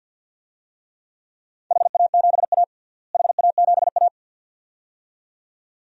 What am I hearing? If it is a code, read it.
Morse code, heard as a keyed tone, about 49 wpm, 700 Hz. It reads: HA7A HA7A
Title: DXCC-100 Call Signs - 4 Characters - Repeated 2x - 50wpm